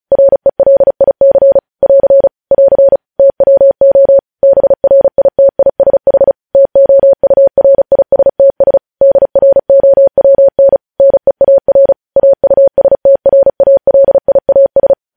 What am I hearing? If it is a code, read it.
Morse code, heard as a keyed tone, about 36 wpm, 550 Hz. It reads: RELIC <AR> <AR> TWO BRITISH TOURISTS DROWN NEAR AUSTRALIAS
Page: news headlines in morse code at 35 wpm